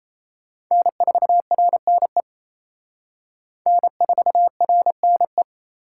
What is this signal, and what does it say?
Morse code, heard as a keyed tone, about 34 wpm, 700 Hz. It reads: N4RNE N4RNE
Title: Speed Racing - USA 1x3 Call Signs 28wpm